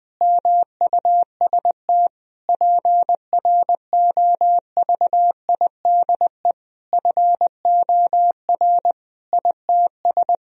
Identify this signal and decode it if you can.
Morse code, heard as a keyed tone, about 20 wpm, 700 Hz. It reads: MUST PROVIDE FOR ITS
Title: Sentences from Top 1000 Words - Review Part 3 - 20wpm